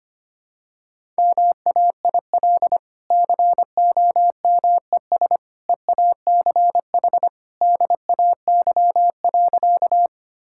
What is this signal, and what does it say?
Morse code, heard as a keyed tone, about 25 wpm, 700 Hz. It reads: MAIL COMES EACH DAY.
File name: Easy Sentences with Hints - 25wpm